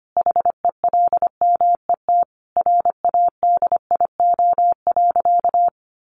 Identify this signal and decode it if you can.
Morse code, heard as a keyed tone, about 25 wpm, 700 Hz. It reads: HELMET RADIO.
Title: A Sound Of Thunder 25wpm